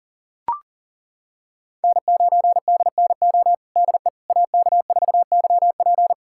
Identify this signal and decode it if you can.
Morse code, heard as a keyed tone, about 40 wpm, 700 Hz. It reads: N9DNO DE AK4YP
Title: USA Callsign DE Callsign - 5 Characters - 40wpm